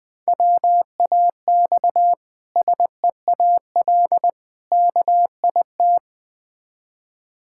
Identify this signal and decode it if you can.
Morse code, heard as a keyed tone, about 21 wpm, 700 Hz. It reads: WAX SEAL KIT
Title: Difficult Words with Hints - 20wpm